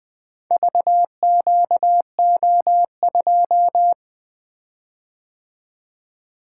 Morse code, heard as a keyed tone, about 20 wpm, 700 Hz. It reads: VQO2
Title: Four Letter-Number Combinations - 20wpm